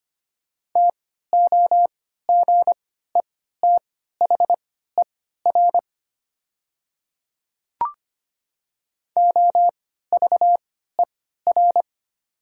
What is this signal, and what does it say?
Morse code, heard as a keyed tone, about 25 wpm, spaced as for 14 wpm, 690 Hz. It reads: TOGETHER OVER
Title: 3x Character Spacing - Top 300 Words - 25wpm